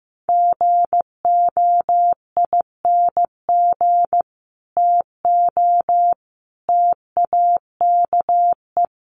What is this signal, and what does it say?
Morse code, heard as a keyed tone, about 15 wpm, 700 Hz. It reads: GOING TO TAKE
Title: Sets of 3 Words 15wpm